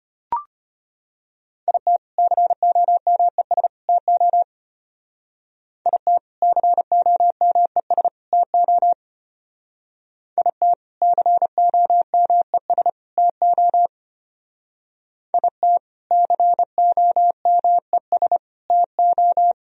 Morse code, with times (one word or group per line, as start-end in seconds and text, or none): 1.7-2.0 IT
2.2-3.7 COMES
3.9-4.4 TO
5.9-6.2 IT
6.4-8.1 COMES
8.3-8.9 TO
10.4-10.7 IT
11.0-12.9 COMES
13.2-13.9 TO
15.3-15.8 IT
16.1-18.4 COMES
18.7-19.5 TO